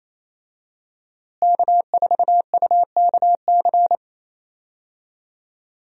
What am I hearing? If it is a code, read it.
Morse code, heard as a keyed tone, about 28 wpm, 700 Hz. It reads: K4UKC
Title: USA Call Signs - 5 Characters - Repeated 3x - 28wpm